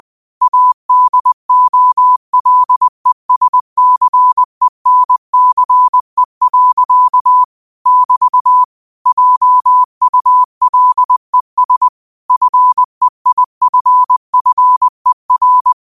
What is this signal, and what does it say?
Morse code, heard as a keyed tone, about 20 wpm, 1000 Hz. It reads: ADOLESCENCE. <BT> JULES FEIFFER